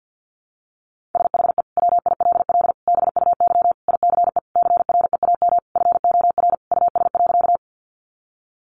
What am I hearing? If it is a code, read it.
Morse code, heard as a keyed tone, about 50 wpm, 700 Hz. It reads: SHE WILL BUY ICE CREAM FOR US.
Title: Sentences from Top 1500 Words 50wpm